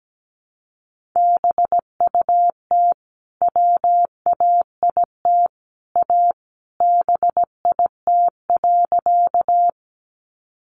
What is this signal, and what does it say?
Morse code, heard as a keyed tone, about 17 wpm, 700 Hz. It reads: BUT WAIT A BIT.
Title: Sentences from Top 900 Words 17wpm